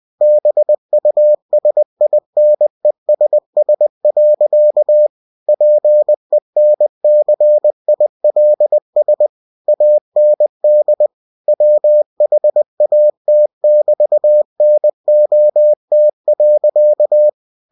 Morse code, heard as a keyed tone, about 20 wpm, 600 Hz. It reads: BUSINESS. PENCILS AND WHAT<BT>NOT.
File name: cw-books / winnie-the-pooh / winnie-20wpm-page-0109